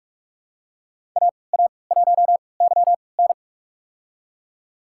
Morse code, heard as a keyed tone, about 47 wpm, spaced as for 26 wpm, 700 Hz. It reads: AA1YN
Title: USA Call Signs - 5 Characters - 3x Character Spacing - 45wpm